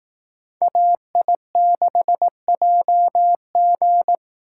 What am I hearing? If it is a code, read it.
Morse code, heard as a keyed tone, about 18 wpm, 700 Hz. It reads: AI6JG
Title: Speed Racing - USA 2x2 Call Signs 15wpm